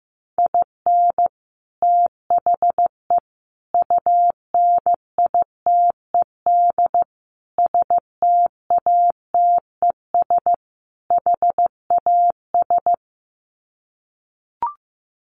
Morse code, heard as a keyed tone, about 15 wpm, 700 Hz. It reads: IN THE UNITED STATES HAS
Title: Sets of 5 Words Encore2 15wpm